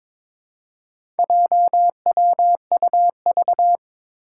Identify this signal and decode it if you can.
Morse code, heard as a keyed tone, about 22 wpm, 700 Hz. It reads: JWUV
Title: Four Challenging Letter-Number Combinations 22wpm